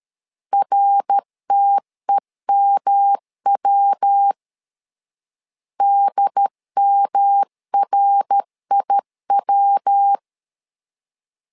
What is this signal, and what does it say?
Morse code, heard as a keyed tone, about 13 wpm, 800 Hz. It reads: RTEMW DMRIW
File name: LEZIONI / 3 MWRO / 3 MWR 12wpm 1711